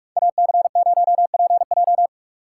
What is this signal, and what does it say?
Morse code, heard as a keyed tone, about 47 wpm, 700 Hz. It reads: AC0PJ
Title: Speed Racing - USA Call Signs 45wpm